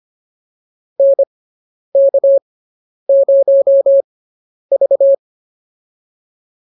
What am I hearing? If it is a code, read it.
Morse code, heard as a keyed tone, about 25 wpm, spaced as for 10 wpm, 550 Hz. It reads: NK0V